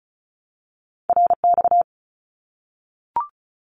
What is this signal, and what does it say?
Morse code, heard as a keyed tone, about 36 wpm, 700 Hz. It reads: RX
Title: Abbreviations 35wpm